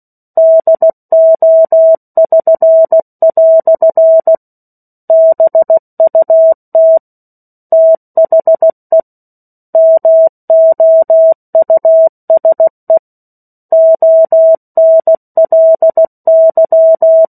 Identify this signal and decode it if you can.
Morse code, heard as a keyed tone, about 16 wpm, 650 Hz. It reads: DO<SN>" BUT THE MOUSE ONLY
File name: 03-Chapter-03 16 WPM